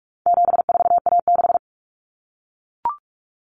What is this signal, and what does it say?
Morse code, heard as a keyed tone, about 44 wpm, 700 Hz. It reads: 74A6
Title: Common Cut Numbers - 4-Digit 45wpm